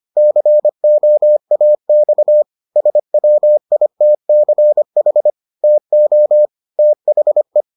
Morse code, heard as a keyed tone, about 25 wpm, 600 Hz. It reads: COAX SWITCH TO THE